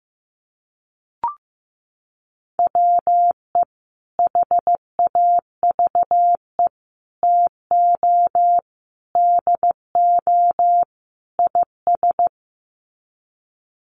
Morse code, heard as a keyed tone, about 15 wpm, 700 Hz. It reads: WE HAVE TO DO IS
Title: Sets of 5 Words Encore 15wpm